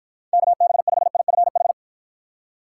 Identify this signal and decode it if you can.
Morse code, heard as a keyed tone, about 49 wpm, 700 Hz. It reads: KB5IFH